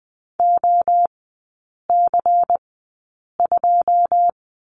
Morse code, heard as a keyed tone, about 20 wpm, spaced as for 8 wpm, 700 Hz. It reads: OC2
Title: 100 Lesson 17e - Letter u - Three Characters 20wpm